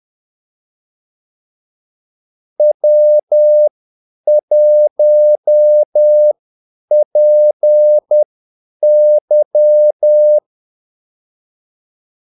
Morse code, heard as a keyed tone, about 10 wpm, spaced as for 8 wpm, 600 Hz. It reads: W1PY
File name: Callsigns-10-8wpm